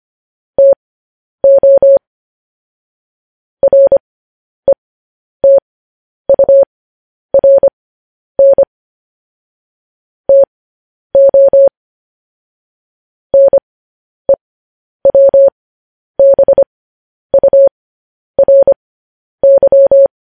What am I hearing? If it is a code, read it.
Morse code, heard as a keyed tone, about 25 wpm, spaced as for 10 wpm, 550 Hz. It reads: TO RETURN TO NEWBURY
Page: news headlines in morse code at 10 wpm